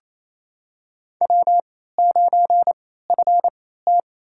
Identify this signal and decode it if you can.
Morse code, heard as a keyed tone, about 28 wpm, spaced as for 16 wpm, 700 Hz. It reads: W9FT
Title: USA Call Signs - 3x Character Spacing - 28wpm